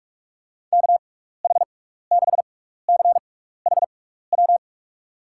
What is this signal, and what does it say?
Morse code, heard as a keyed tone, about 43 wpm, spaced as for 15 wpm, 700 Hz. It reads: KH6CHW
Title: USA Call Signs - 6 Characters - 6x Character Spacing - 45wpm